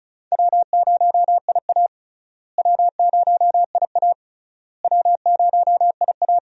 Morse code, heard as a keyed tone, about 36 wpm, 700 Hz. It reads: W0IA W0IA W0IA
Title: USA Call Signs - 4 Characters - Repeated 3x - 35wpm